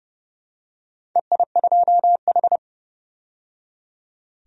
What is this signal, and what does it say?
Morse code, heard as a keyed tone, about 30 wpm, 700 Hz. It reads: EI2H
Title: DXCC-100 Call Signs - 4 Characters - 30wpm